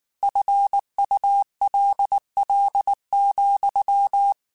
Morse code, heard as a keyed tone, about 19 wpm, 800 Hz. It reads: FULL,
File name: Chuckles / Chuckles-mp3s / Glass Half Full @18